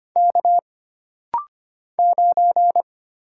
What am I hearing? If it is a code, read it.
Morse code, heard as a keyed tone, about 25 wpm, 700 Hz. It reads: K 9
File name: Single Character (Letter:Number:Punctuation:Prosign) - Rapid-Fire - 25wpm